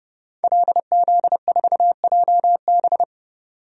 Morse code, heard as a keyed tone, about 30 wpm, 700 Hz. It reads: LZ4JB
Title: DXCC-100 Call Signs - 5 Characters - 30wpm